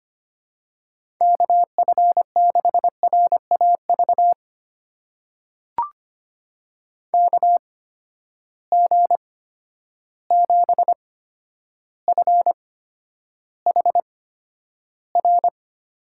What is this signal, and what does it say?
Morse code, heard as a keyed tone, about 25 wpm, spaced as for 11 wpm, 700 Hz. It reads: KF6RAV KG7FHR
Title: USA Call Signs - 6 Characters - 8x Character Spacing - 25wpm